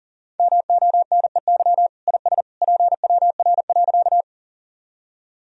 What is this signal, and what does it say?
Morse code, heard as a keyed tone, about 40 wpm, 700 Hz. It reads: MONEY IS PWR.
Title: Sentences from Top 500 Words and Abbreviations 40wpm